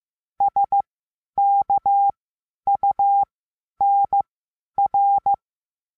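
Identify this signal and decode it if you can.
Morse code, heard as a keyed tone, about 15 wpm, spaced as for 10 wpm, 800 Hz. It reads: SKUNR
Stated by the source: Morse lesson adding the letter N, sent as five-letter code groups.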